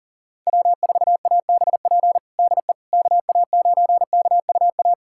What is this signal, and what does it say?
Morse code, heard as a keyed tone, about 40 wpm, 700 Hz. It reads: W4ABP DE KA9KUA
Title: USA Callsign DE Callsign 40wpm